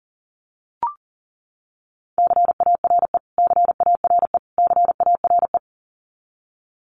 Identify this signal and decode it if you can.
Morse code, heard as a keyed tone, about 40 wpm, 700 Hz. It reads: CARE CARE CARE
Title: Top 500 Words Repeated 40wpm